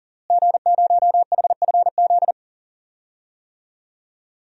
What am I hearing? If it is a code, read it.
Morse code, heard as a keyed tone, about 40 wpm, 700 Hz. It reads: G0HFZ
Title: DXCC-100 Call Signs - 5 Characters - Repeated 3x - 40wpm